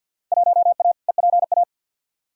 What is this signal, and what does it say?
Morse code, heard as a keyed tone, about 52 wpm, 700 Hz. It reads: 1A EPA
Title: Field Day Exchanges - 50wpm